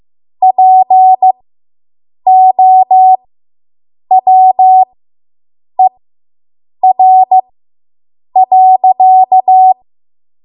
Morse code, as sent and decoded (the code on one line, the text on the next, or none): .--. --- .-- . .-. .-.-.-
POWER.